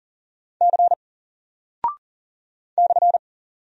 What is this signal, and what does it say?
Morse code, heard as a keyed tone, about 39 wpm, 700 Hz. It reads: C /
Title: Single Character (Letter:Number:Punctuation:Prosign) Alt 40wpm